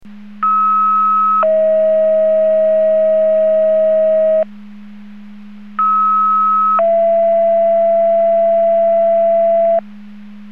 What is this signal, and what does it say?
Two-tone page, heard as a one-second tone, then a three-second tone, about 1290 then 640 Hz, about 1290 then 670 Hz. It is heard from 0.4 to 9.8 s.